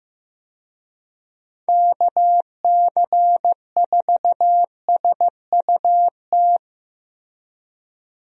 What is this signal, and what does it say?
Morse code, heard as a keyed tone, about 15 wpm, 700 Hz. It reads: KC4SUT
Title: USA Call Signs - 6 Characters - Repeated 2x - 15wpm